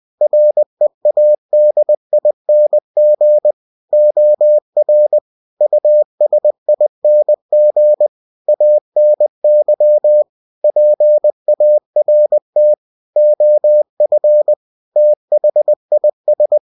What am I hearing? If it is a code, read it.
Morse code, heard as a keyed tone, about 20 wpm, 600 Hz. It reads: READING OR USING ANY PART OF THIS
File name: cw-books / winnie-the-pooh / winnie-20wpm-page-0113